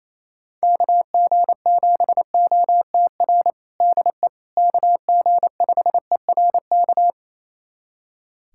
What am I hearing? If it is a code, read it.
Morse code, heard as a keyed tone, about 27 wpm, 700 Hz. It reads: KG7OTR DE KG5ERK
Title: USA Callsign DE Callsign - 6 Characters - 28wpm